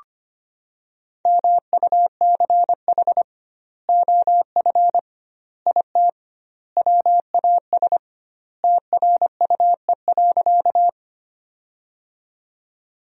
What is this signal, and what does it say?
Morse code, heard as a keyed tone, about 25 wpm, 700 Hz. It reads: MUCH OF IT WAS TRUE.
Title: 2x Word-Spacing - Sentences from Top 400 Words - 25wpm